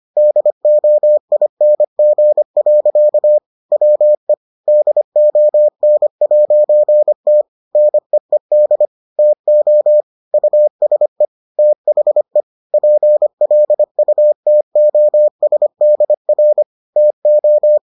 Morse code, heard as a keyed tone, about 25 wpm, 600 Hz. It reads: DOING. WE DON'T NEED TO USE THE PLUTOSDR TO